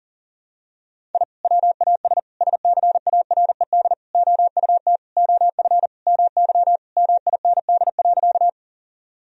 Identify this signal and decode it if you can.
Morse code, heard as a keyed tone, about 40 wpm, 700 Hz. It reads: I WAS SCARED OUT OF MY MIND.
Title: Sentences from Top 1000 Words - Review Part 4 - 40wpm